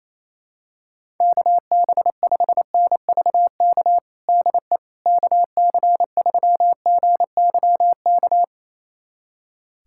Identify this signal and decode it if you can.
Morse code, heard as a keyed tone, about 28 wpm, 700 Hz. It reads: KB5NVK DE KC3GYK
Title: USA Callsign DE Callsign 28wpm